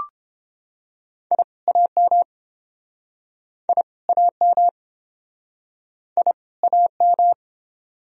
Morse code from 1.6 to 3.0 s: AM

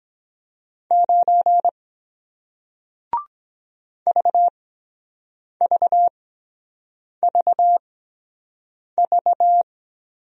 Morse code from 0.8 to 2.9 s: 9

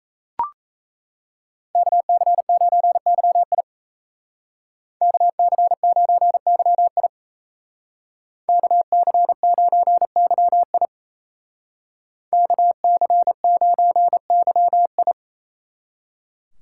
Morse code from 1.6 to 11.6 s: KC9YI KC9YI KC9YI

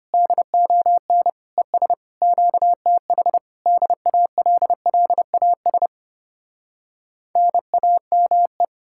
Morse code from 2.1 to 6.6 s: QTH DALLAS

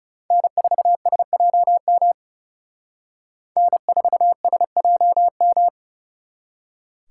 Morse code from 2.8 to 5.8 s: N4SJM

